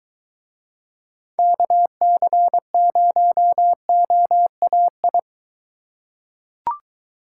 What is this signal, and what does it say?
Morse code, heard as a keyed tone, about 21 wpm, 700 Hz. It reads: KC0OAI